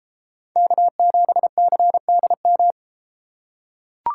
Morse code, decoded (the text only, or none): K7CDM